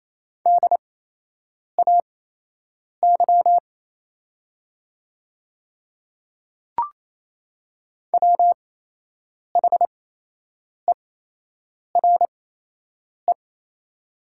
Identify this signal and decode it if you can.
Morse code, heard as a keyed tone, about 28 wpm, spaced as for 8 wpm, 700 Hz. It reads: DAY WHERE
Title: 8x Character Spacing - Top 200 Words - 28wpm